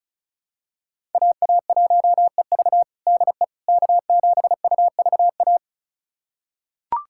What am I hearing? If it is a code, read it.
Morse code, heard as a keyed tone, about 36 wpm, 700 Hz. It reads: AA1EV DE K7UVA
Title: USA Callsign DE Callsign - 5 Characters - 35wpm